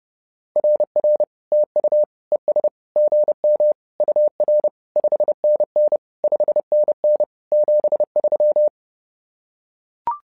Morse code, heard as a keyed tone, about 30 wpm, 600 Hz. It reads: RR TU ES GM UR 5NN 5NN 73
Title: SOTA QSO Practice 30wpm